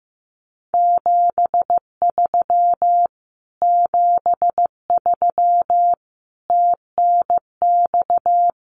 Morse code, heard as a keyed tone, about 15 wpm, 700 Hz. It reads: 73 73 TNX
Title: Common QSO Phrases Encore 15wpm